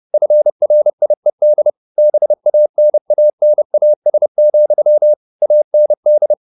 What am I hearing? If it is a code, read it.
Morse code, heard as a keyed tone, about 30 wpm, 600 Hz. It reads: FRIED BANANAS, AND
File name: the old man and the sea / the old man and the sea 1 30wpm